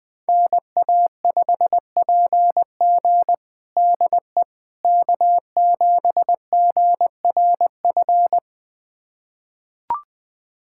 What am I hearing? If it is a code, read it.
Morse code, heard as a keyed tone, about 20 wpm, 700 Hz. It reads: NA5PG DE K7GRF
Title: USA Callsign DE Callsign - 5 Characters - 20wpm